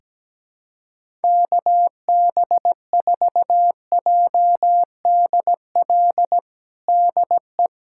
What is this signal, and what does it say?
Morse code, heard as a keyed tone, about 17 wpm, 700 Hz. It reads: KB4JDL DE
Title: USA Callsign DE Callsign 17wpm